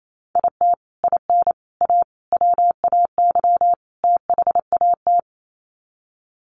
Morse code, heard as a keyed tone, about 28 wpm, 700 Hz. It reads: IT IN A WAY THAT
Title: Sets of 5 Words Encore2 28wpm